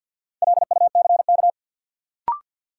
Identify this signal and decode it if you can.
Morse code, heard as a keyed tone, about 52 wpm, 700 Hz. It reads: LUCK
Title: Sentences from Top 1500 Words - New Word Drill - 50wpm